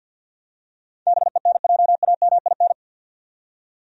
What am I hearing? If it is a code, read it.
Morse code, heard as a keyed tone, about 52 wpm, 700 Hz. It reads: BENJAMIN
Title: USA Common Names - 50wpm